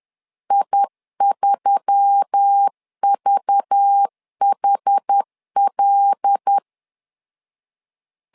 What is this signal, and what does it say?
Morse code, heard as a keyed tone, about 10 wpm, 800 Hz. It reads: I3VHL
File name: LEZIONI / 5 LBVC / 5 LBVC 10wpm 2349